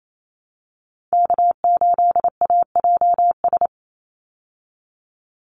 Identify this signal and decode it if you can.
Morse code, heard as a keyed tone, about 28 wpm, 700 Hz. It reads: K8AJS
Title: Call Signs - 28wpm